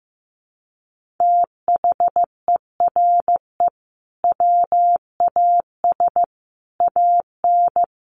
Morse code, heard as a keyed tone, about 15 wpm, 700 Hz. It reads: THERE WAS AN
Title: Sentences from Top 1400 Words 15wpm